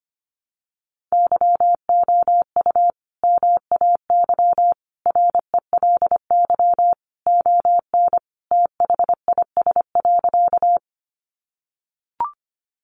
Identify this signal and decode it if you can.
Morse code, heard as a keyed tone, about 25 wpm, 700 Hz. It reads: YOU MAY RELY ON THIS.
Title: Sentences from Top 1200 Words 25wpm